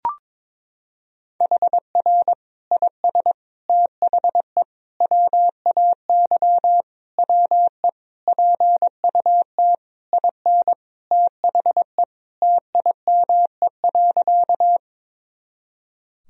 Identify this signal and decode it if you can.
Morse code, heard as a keyed tone, about 22 wpm, 700 Hz. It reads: HR IS THE WAY WE PUT IN THE TIME.
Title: Sentences from Top 200 Words and Abbreviations 22wpm